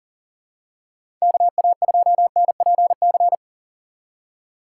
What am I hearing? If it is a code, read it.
Morse code, heard as a keyed tone, about 40 wpm, 700 Hz. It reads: KA2NPC
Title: USA Call Signs - 6 Characters - 3x Character Spacing - 40wpm